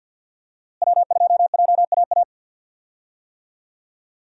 Morse code, heard as a keyed tone, about 51 wpm, 700 Hz. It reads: W2JAA
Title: USA Call Signs - 5 Characters - Repeated 2x - 50wpm